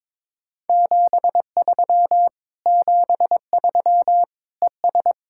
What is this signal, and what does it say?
Morse code, heard as a keyed tone, about 22 wpm, 700 Hz. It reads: 73 73 ES
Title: Speed-Racing Common QSO Phrases Encore - 22wpm